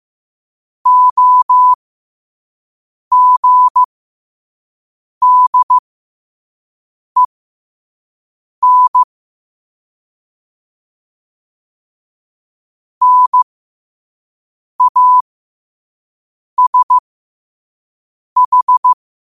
Morse code, heard as a keyed tone, about 15 wpm, spaced as for 5 wpm, 1000 Hz. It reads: OGDEN NASH
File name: qotdfiles / 5wpm